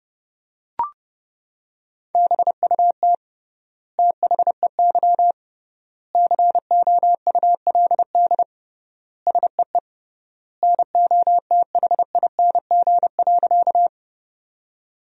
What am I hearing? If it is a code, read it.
Morse code, heard as a keyed tone, about 30 wpm, 700 Hz. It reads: BUT THEY COULD SEE NOTHING.
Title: 3x Word-Spacing - Sentences from Top 500 Words - 30wpm